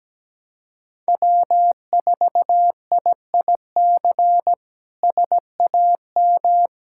Morse code, heard as a keyed tone, about 17 wpm, 700 Hz. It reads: W4IIC SAM
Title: K1USN SST QSO Practice - 17wpm